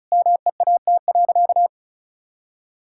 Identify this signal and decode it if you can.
Morse code, heard as a keyed tone, about 34 wpm, 700 Hz. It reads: MEAT.
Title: Sentences from Top 1600 Words 35wpm